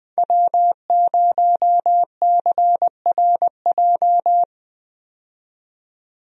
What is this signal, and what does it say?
Morse code, heard as a keyed tone, about 20 wpm, 700 Hz. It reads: W0CRJ
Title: USA Call Signs - 5 Characters - 20wpm